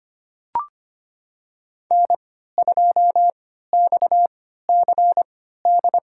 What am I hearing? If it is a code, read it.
Morse code, heard as a keyed tone, about 25 wpm, spaced as for 14 wpm, 700 Hz. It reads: N2XCD
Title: USA Call Signs - 5 Characters - 3x Character Spacing - 25wpm